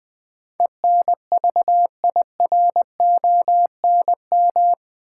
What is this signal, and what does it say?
Morse code, heard as a keyed tone, about 20 wpm, 700 Hz. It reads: ENVIRONM
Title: Build A Word - 4 Letters At A Time - 11-Letter Words - 20wpm